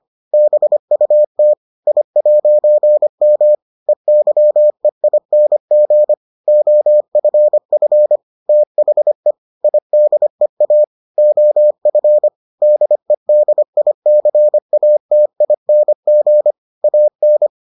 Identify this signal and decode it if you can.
Morse code, heard as a keyed tone, about 25 wpm, 600 Hz. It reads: BUT I'M EYEING OFF THE IDEA OF DEDICATING AN